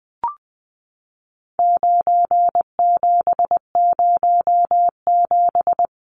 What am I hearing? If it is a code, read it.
Morse code, heard as a keyed tone, about 19 wpm, 700 Hz. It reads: 9707
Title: Numbers - 4-Digit 20wpm